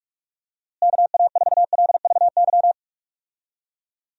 Morse code, heard as a keyed tone, about 42 wpm, 700 Hz. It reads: KA4LVY
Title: USA Call Signs - 6 Characters - Repeated 2x - 45wpm